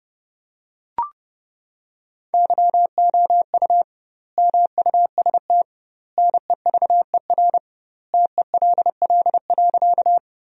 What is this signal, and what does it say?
Morse code, heard as a keyed tone, about 30 wpm, 700 Hz. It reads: YOU MUST NEVER TELL.
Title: 2x Word-Spacing - Sentences from Top 300 Words - 30wpm